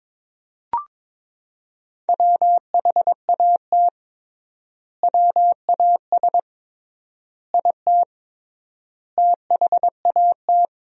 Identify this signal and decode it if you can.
Morse code, heard as a keyed tone, about 22 wpm, 700 Hz. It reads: WHAT WAS IT THAT
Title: 3x Word-Spacing - Sentences from Top 300 Words - 22wpm